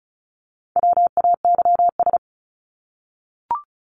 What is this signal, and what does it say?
Morse code, heard as a keyed tone, about 34 wpm, 700 Hz. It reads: WAYS
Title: ES:S Suffix - Top 100 Words with Base Words 35wpm